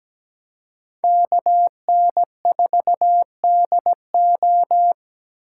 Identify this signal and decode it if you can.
Morse code, heard as a keyed tone, about 20 wpm, 700 Hz. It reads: KN4DO
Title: Speed Racing - USA 2x2 Call Signs 17wpm